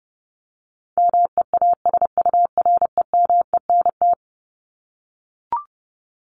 Morse code, heard as a keyed tone, about 30 wpm, 700 Hz. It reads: MEASUREMENT
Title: Top 1751-2000 Words 30wpm